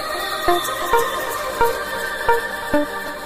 There is no code.